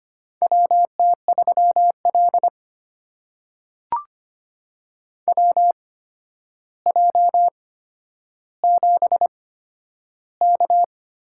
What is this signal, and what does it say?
Morse code, heard as a keyed tone, about 25 wpm, spaced as for 11 wpm, 700 Hz. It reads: WT3L WJ7K
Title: USA Call Signs - 4 Characters - 8x Character Spacing - 25wpm